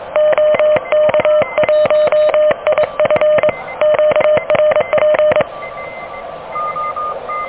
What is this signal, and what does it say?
Morse code, heard as a keyed tone, about 22 wpm, 610 Hz. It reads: OK1IF QRP
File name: OK QRP 2023 / OK QRP 2023